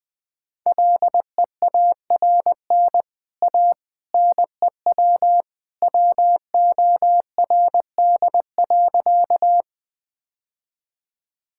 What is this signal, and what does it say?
Morse code, heard as a keyed tone, about 20 wpm, 700 Hz. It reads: LEARN A NEW WORD.
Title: Easy Sentences with Hints - 20wpm